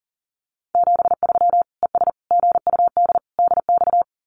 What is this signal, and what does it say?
Morse code, heard as a keyed tone, about 40 wpm, 700 Hz. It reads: 73 ES GUD DX
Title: Common QSO Phrases 40wpm